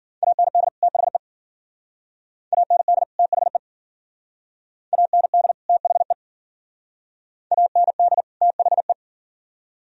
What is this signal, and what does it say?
Morse code, heard as a keyed tone, about 50 wpm, 700 Hz. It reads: AND THE AND THE AND THE AND THE
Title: Speed-Racing - Sets of 2 Words 40wpm